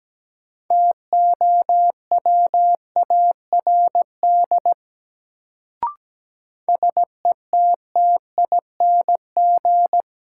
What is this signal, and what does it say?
Morse code, heard as a keyed tone, about 17 wpm, 700 Hz. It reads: TOWARD SETTING
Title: Sentences from Top 1300 Words - New Word Drill Alt - 17wpm